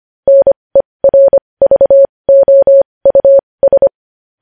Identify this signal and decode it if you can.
Morse code, heard as a keyed tone, about 25 wpm, spaced as for 20 wpm, 550 Hz. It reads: NERVOUS
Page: news headlines in morse code at 20 wpm